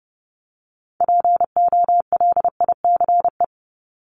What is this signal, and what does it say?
Morse code, heard as a keyed tone, about 30 wpm, 700 Hz. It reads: POLICE
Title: Top 1751-2000 Words 30wpm